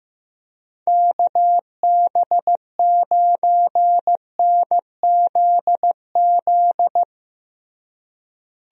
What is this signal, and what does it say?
Morse code, heard as a keyed tone, about 16 wpm, 700 Hz. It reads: KB9NZZ
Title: USA Call Signs - 6 Characters - Repeated 2x - 15wpm